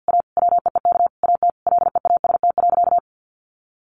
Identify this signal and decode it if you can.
Morse code, heard as a keyed tone, about 50 wpm, 700 Hz. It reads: A WEEK AT LEAST.
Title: Sentences from Top 900 Words 50wpm